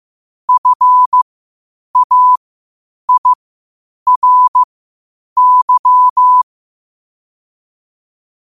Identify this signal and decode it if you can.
Morse code, heard as a keyed tone, about 15 wpm, spaced as for 8 wpm, 1000 Hz. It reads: FAIRY
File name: qotdfiles / 8wpm